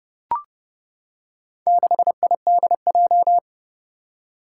Morse code, heard as a keyed tone, about 30 wpm, 700 Hz. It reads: 6IDJ
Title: Four Challenging Letter-Number Combinations 30wpm